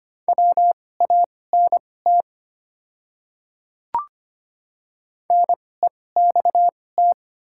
Morse code, heard as a keyed tone, about 25 wpm, spaced as for 18 wpm, 700 Hz. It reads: WANT NEXT
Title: 2x Character Spacing - Top 300 Words - 25wpm